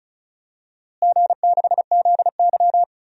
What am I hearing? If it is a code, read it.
Morse code, heard as a keyed tone, about 36 wpm, 700 Hz. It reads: G6ZY
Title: DXCC-100 Call Signs - 4 Characters - Repeated 2x - 35wpm